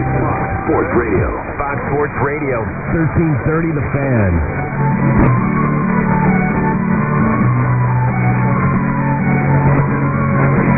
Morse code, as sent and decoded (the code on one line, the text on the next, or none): none